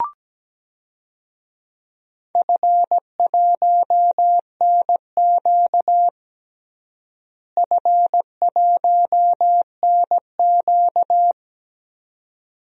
Morse code, heard as a keyed tone, about 17 wpm, 700 Hz. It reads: F1NQ F1NQ
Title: DXCC-100 Call Signs - 4 Characters - Repeated 2x - 17wpm